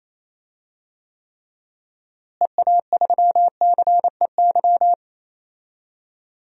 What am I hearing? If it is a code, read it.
Morse code, heard as a keyed tone, about 28 wpm, 700 Hz. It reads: EA3CEY
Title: DXCC-100 Call Signs - 6 Characters - Repeated 2x - 28wpm